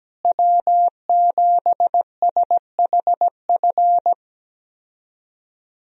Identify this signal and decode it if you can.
Morse code, heard as a keyed tone, about 17 wpm, 700 Hz. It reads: W7SHF
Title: USA Call Signs - 5 Characters - Repeated 2x - 17wpm